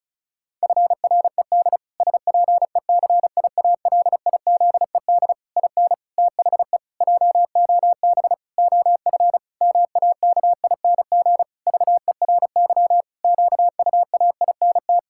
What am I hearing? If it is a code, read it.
Morse code, heard as a keyed tone, about 34 wpm, 700 Hz. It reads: FRED SPECIALIZED IN THE JOB OF MAKING VERY QUAINT
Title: Pangram Challenge v2 - 35wpm